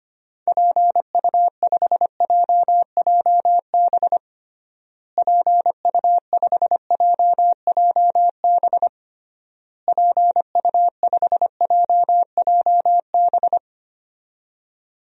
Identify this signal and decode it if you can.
Morse code, heard as a keyed tone, about 25 wpm, 700 Hz. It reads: PU5JJB PU5JJB PU5JJB
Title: DXCC-100 Call Signs - 6 Characters - Repeated 3x - 25wpm